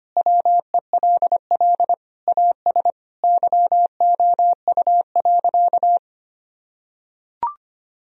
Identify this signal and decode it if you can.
Morse code, heard as a keyed tone, about 25 wpm, 700 Hz. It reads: WELL AS YOU.
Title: Sentences from Top 1000 Words - Review Part 3 - 25wpm